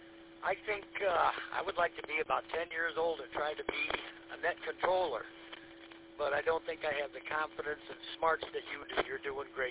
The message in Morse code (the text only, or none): none